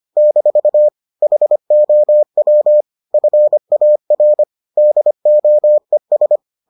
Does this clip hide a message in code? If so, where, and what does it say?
Yes: Morse code from 0.2 to 6.4 s, heard as a keyed tone, about 25 wpm, 600 Hz. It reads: - HOW FAR DOES